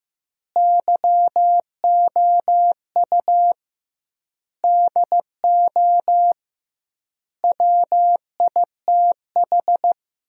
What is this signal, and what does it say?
Morse code, heard as a keyed tone, about 14 wpm, 700 Hz. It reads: YOU DO WITH
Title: 2x Word-Spacing - Sentences from Top 300 Words - 15wpm